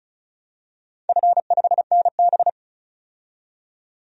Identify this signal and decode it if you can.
Morse code, heard as a keyed tone, about 36 wpm, 700 Hz. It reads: F5NB
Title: DXCC-100 Call Signs - 4 Characters - 35wpm